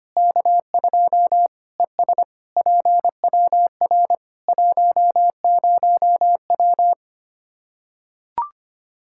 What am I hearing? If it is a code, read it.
Morse code, heard as a keyed tone, about 25 wpm, 700 Hz. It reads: K2 ES PWR 10W
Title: Practice QSO - Exchange 2 - Info - 25wpm